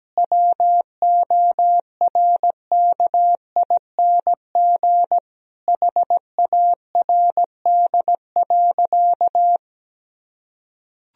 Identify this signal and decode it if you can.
Morse code, heard as a keyed tone, about 17 wpm, 700 Hz. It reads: WORKING HARD.
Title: Sentences from Top 1000 Words - Review Part 4 - 17wpm